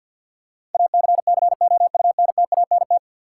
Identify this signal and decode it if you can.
Morse code, heard as a keyed tone, about 52 wpm, 700 Hz. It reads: ACCOUNTANT
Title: Practice QSO - Exchange 3 - More Info - 50wpm